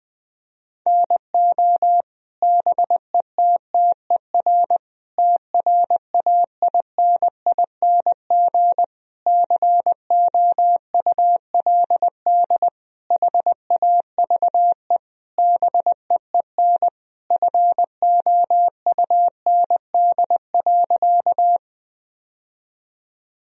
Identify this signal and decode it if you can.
Morse code, heard as a keyed tone, about 20 wpm, 700 Hz. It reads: NO BETTER TRAINING COULD HAVE BEEN FOUND.
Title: Sentences from Top 1000 Words - Review Part 4 - 20wpm